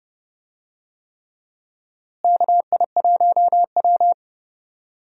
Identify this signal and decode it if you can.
Morse code, heard as a keyed tone, about 32 wpm, 700 Hz. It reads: KI1W
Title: USA Call Signs - 4 Characters - Repeated 3x - 30wpm